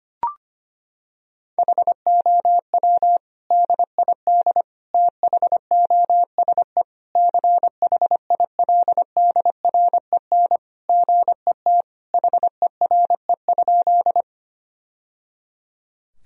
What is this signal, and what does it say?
Morse code, heard as a keyed tone, about 25 wpm, 700 Hz. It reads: HOW DID THOSE CHILDREN GET HERE?
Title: Sentences from Top 1000 Words - Review Part 1 - 25wpm